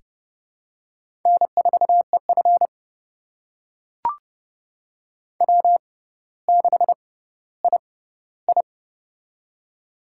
Morse code, heard as a keyed tone, about 30 wpm, spaced as for 15 wpm, 700 Hz. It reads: N4EF W6II
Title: USA Call Signs - 4 Characters - 6x Character Spacing - 30wpm